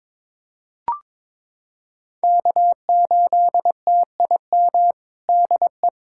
Morse code, heard as a keyed tone, about 22 wpm, 700 Hz. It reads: K8TIM DE
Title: USA Callsign DE Callsign - 5 Characters - 22wpm